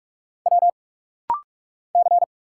Morse code, heard as a keyed tone, about 46 wpm, 690 Hz. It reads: W C
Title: Single Letters - Rapid-Fire 45wpm